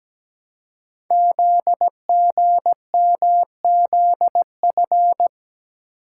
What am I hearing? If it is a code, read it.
Morse code, heard as a keyed tone, about 17 wpm, 700 Hz. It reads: ZGMZF
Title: Code Groups - 5 Letters - 17wpm